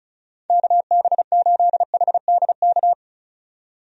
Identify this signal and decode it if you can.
Morse code, heard as a keyed tone, about 34 wpm, 700 Hz. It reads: KB8HDK
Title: USA Call Signs - 6 Characters - 35wpm